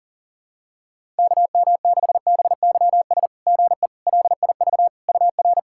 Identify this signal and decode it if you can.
Morse code, heard as a keyed tone, about 40 wpm, 690 Hz. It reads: KM6BYS GE LIV UR
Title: SOTA QSO Practice 40wpm